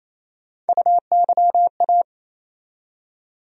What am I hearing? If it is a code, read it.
Morse code, heard as a keyed tone, about 28 wpm, 700 Hz. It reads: UYA
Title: USA Call Sign Postfixes - 3 Letters 28wpm